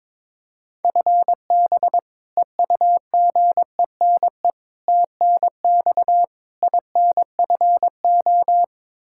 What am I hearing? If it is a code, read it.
Morse code, heard as a keyed tone, about 22 wpm, 700 Hz. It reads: FB EUGENE TNX INFO
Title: Practice QSO - Exchange 3 - More Info - 22wpm